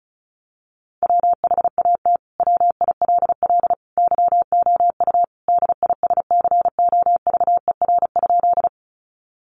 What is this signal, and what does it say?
Morse code, heard as a keyed tone, about 34 wpm, 700 Hz. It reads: WHAT WILL YOU DISCOVER?
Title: Sentences from Top 1100 Words 35wpm